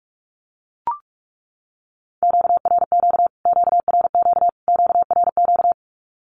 Morse code, heard as a keyed tone, about 44 wpm, 700 Hz. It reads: QRQ QRQ QRQ
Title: Q-Codes Repeated 3x 45wpm